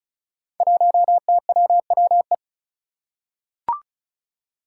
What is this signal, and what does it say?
Morse code, heard as a keyed tone, about 36 wpm, 700 Hz. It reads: 1TWWE
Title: Five Challenging Letter-Number Combinations 35wpm